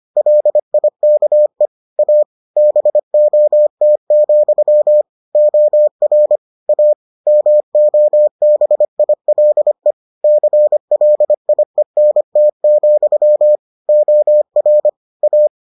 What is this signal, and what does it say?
Morse code, heard as a keyed tone, about 25 wpm, 600 Hz. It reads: LIKE A BOT, OR A MOBILE CLIENT, OR A